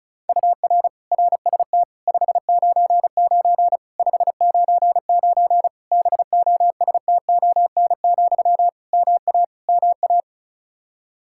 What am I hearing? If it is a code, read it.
Morse code, heard as a keyed tone, about 36 wpm, 700 Hz. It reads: UR RST 599 599 BOSTON, MA MA
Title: Practice QSO - Exchange 1 - Beginning - 35wpm